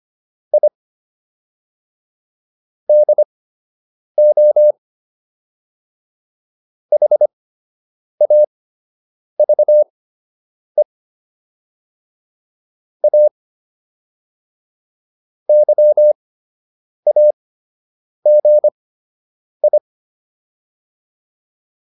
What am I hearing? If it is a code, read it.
Morse code, heard as a keyed tone, about 25 wpm, spaced as for 8 wpm, 600 Hz. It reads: I DO HAVE A YAGI